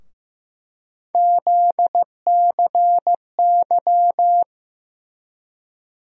Morse code, heard as a keyed tone, about 15 wpm, 700 Hz. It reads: ZCY
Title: USA Call Sign Postfixes - 3 Letters 15wpm